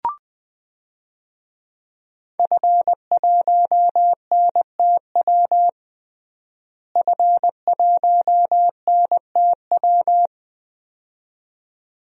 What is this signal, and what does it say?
Morse code, heard as a keyed tone, about 20 wpm, 700 Hz. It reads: F1NTW F1NTW
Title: DXCC-100 Call Signs - 5 Characters - Repeated 2x - 20wpm